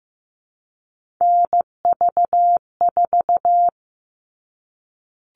Morse code, heard as a keyed tone, about 15 wpm, 700 Hz. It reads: NV4
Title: All Cut Numbers - 3-Digit 15wpm